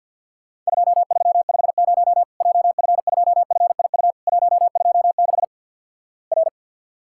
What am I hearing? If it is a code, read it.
Morse code, heard as a keyed tone, about 48 wpm, 700 Hz. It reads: 2350 JF2FIU 126 R
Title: Worked All Europe DX Contest - QTC Exchange with Repeats - 50wpm